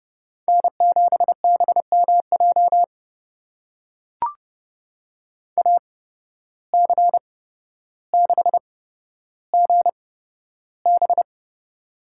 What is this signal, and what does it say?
Morse code, heard as a keyed tone, about 30 wpm, spaced as for 13 wpm, 700 Hz. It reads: N7BMJ AC6GB
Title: USA Call Signs - 5 Characters - 8x Character Spacing - 30wpm